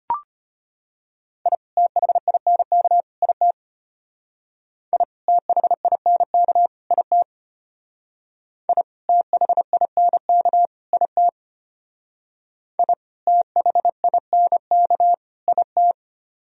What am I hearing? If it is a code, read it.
Morse code, heard as a keyed tone, about 32 wpm, 700 Hz. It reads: I THINK IT I THINK IT I THINK IT I THINK IT